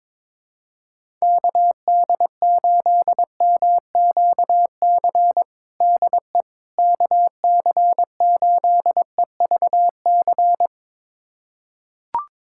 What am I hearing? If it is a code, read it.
Morse code, heard as a keyed tone, about 22 wpm, 700 Hz. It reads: KD8MQC DE KC8EVC
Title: USA Callsign DE Callsign 22wpm